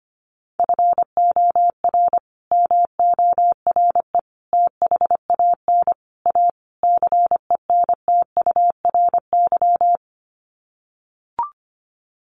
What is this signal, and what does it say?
Morse code, heard as a keyed tone, about 25 wpm, 700 Hz. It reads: FOR MORE THAN A CENTURY
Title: Sets of 5 Words Encore2 25wpm